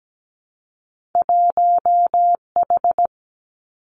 Morse code, heard as a keyed tone, about 17 wpm, 700 Hz. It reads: 1H
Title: Two Challenging Letter-Number Combinations 17wpm